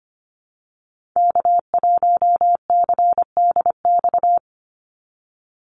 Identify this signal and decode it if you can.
Morse code, heard as a keyed tone, about 25 wpm, 700 Hz. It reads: K1CDX
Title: Call Signs - 25wpm